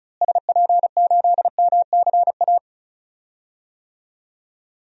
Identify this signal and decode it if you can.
Morse code, heard as a keyed tone, about 36 wpm, 700 Hz. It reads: SP8MCA
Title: DXCC-100 Call Signs - 6 Characters - Repeated 3x - 35wpm